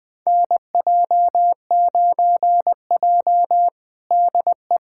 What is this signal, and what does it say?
Morse code, heard as a keyed tone, about 20 wpm, 700 Hz. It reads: NJ9J DE
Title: USA Callsign DE Callsign - 4 Characters - 20wpm